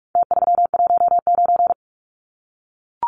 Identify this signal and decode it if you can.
Morse code, heard as a keyed tone, about 44 wpm, 700 Hz. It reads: T319
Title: Numbers - Contest Style - 4-Digit 45wpm